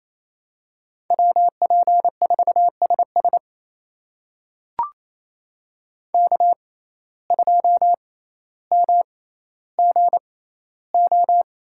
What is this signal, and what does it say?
Morse code, heard as a keyed tone, about 28 wpm, spaced as for 14 wpm, 700 Hz. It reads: WP4SS K2MGO
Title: USA Call Signs - 5 Characters - 6x Character Spacing - 28wpm